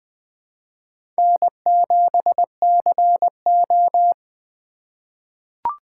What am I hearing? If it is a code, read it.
Morse code, heard as a keyed tone, about 20 wpm, 700 Hz. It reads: N7CO
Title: USA Call Signs - 4 Characters - 2x Character Spacing - 20wpm